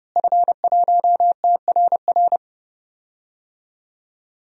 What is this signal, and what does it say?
Morse code, heard as a keyed tone, about 32 wpm, 700 Hz. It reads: F1TRR
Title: DXCC-100 Call Signs - 5 Characters - Repeated 3x - 30wpm